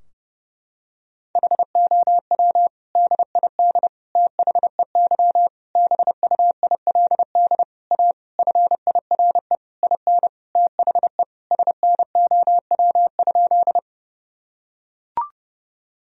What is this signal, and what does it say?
Morse code, heard as a keyed tone, about 32 wpm, 700 Hz. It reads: HOW DID THEY BUILD A FIRE IN THE SNOW?
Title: Sentences from Top 1000 Words - Review Part 3 - 30wpm